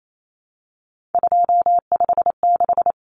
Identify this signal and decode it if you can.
Morse code, heard as a keyed tone, about 28 wpm, 700 Hz. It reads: 256
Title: Common Cut Numbers - 3-Digit 28wpm